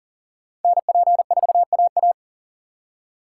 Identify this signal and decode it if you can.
Morse code, heard as a keyed tone, about 50 wpm, 700 Hz. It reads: NP4AA
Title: Speed Racing - USA 2x2 Call Signs 40wpm